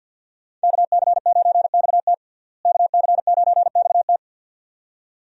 Morse code, heard as a keyed tone, about 51 wpm, 700 Hz. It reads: KC9XT KC9XT
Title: USA Call Signs - 5 Characters - Repeated 2x - 50wpm